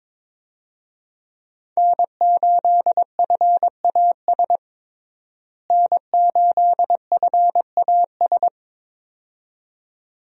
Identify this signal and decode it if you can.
Morse code, heard as a keyed tone, about 22 wpm, 700 Hz. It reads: N8FAS N8FAS
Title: USA Call Signs - 5 Characters - Repeated 2x - 22wpm